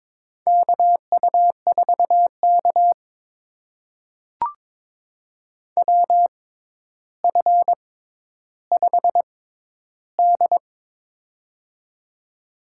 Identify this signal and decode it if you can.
Morse code, heard as a keyed tone, about 22 wpm, spaced as for 11 wpm, 700 Hz. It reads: KU4K WF5D
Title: USA Call Signs - 4 Characters - 6x Character Spacing - 22wpm